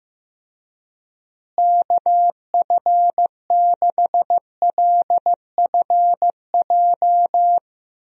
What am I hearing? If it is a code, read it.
Morse code, heard as a keyed tone, about 15 wpm, 700 Hz. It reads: KF6LFJ
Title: USA Call Signs - 6 Characters - Repeated 2x - 15wpm